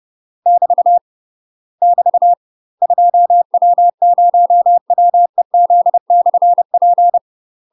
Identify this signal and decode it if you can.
Morse code, heard as a keyed tone, about 30 wpm, 700 Hz. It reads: <BT> <BT> 2W0WEZ/P
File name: UK MASTER QSOs TRAINING / 30WPM / UK MASTER QSOs 0045